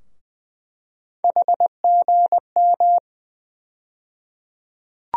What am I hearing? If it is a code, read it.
Morse code, heard as a keyed tone, about 20 wpm, 700 Hz. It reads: HGM
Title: USA Call Sign Postfixes - 3 Letters 20wpm